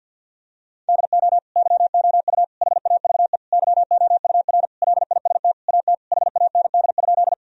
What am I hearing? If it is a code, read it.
Morse code, heard as a keyed tone, about 50 wpm, 700 Hz. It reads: DO YOU HAVE YOUR LIST AT HAND?
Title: Sentences from Top 1000 Words - Review Part 2 - 50wpm